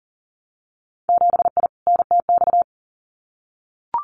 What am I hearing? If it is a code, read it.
Morse code, heard as a keyed tone, about 42 wpm, 700 Hz. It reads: 7I NTX
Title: Winter Field Day Exchanges Encore 40wpm